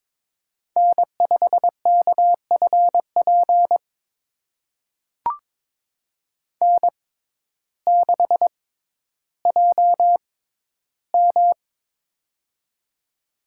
Morse code, heard as a keyed tone, about 22 wpm, spaced as for 12 wpm, 700 Hz. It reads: N5KFP N6JM
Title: USA Call Signs - 6x Character Spacing - 22wpm